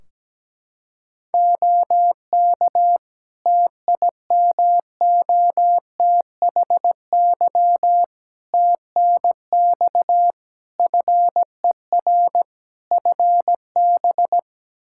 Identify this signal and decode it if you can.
Morse code, heard as a keyed tone, about 17 wpm, 700 Hz. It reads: OK TIMOTHY TNX FER FB
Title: K1USN SST QSO Practice 17wpm